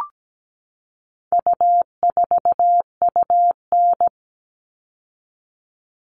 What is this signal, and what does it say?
Morse code, heard as a keyed tone, about 17 wpm, 700 Hz. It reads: U4UN
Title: All Cut Numbers - 4-Digit 17wpm